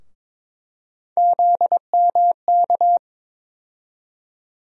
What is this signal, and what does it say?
Morse code, heard as a keyed tone, about 21 wpm, 700 Hz. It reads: ZMK
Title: USA Call Sign Postfixes - 3 Letters 22wpm